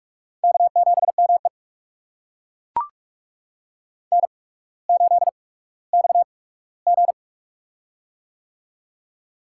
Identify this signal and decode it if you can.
Morse code, heard as a keyed tone, about 42 wpm, spaced as for 19 wpm, 700 Hz. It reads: K7ME N8XG
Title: USA Call Signs - 4 Characters - 8x Character Spacing - 45wpm